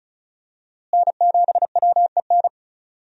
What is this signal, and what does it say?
Morse code, heard as a keyed tone, about 34 wpm, 700 Hz. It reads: N7WEN
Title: USA Call Signs - 5 Characters - Repeated 3x - 35wpm